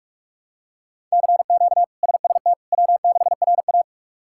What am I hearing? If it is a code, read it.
Morse code, heard as a keyed tone, about 48 wpm, 700 Hz. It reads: CQ SST W6RA
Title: K1USN SST QSO Practice - 45wpm